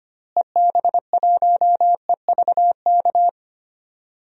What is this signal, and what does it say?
Morse code, heard as a keyed tone, about 25 wpm, 700 Hz. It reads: EB1EVK
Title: DXCC-100 Call Signs - 6 Characters - Repeated 2x - 25wpm